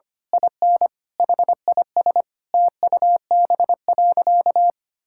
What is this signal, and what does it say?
Morse code, heard as a keyed tone, about 25 wpm, 700 Hz. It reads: IN HIS TUB.
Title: Pangram Challenge v2 - 25wpm